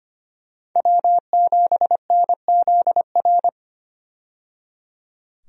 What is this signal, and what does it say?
Morse code, heard as a keyed tone, about 25 wpm, 700 Hz. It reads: W7NZR
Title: USA Call Signs - 5 Characters - 25wpm